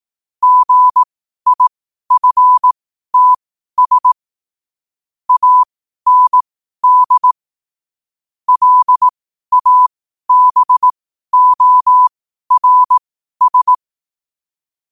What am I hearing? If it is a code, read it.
Morse code, heard as a keyed tone, about 18 wpm, spaced as for 12 wpm, 1000 Hz. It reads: GIFTS AND LABORS
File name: qotdfiles / 12wpm